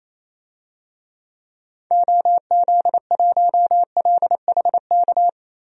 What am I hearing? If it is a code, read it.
Morse code, heard as a keyed tone, about 28 wpm, 700 Hz. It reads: OZ1LHK
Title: DXCC-100 Call Signs - 6 Characters - Repeated 2x - 28wpm